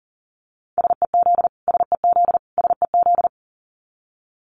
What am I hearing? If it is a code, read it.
Morse code, heard as a keyed tone, about 40 wpm, 700 Hz. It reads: SEZ SEZ SEZ
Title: Abbreviations Repeated 3x - 40wpm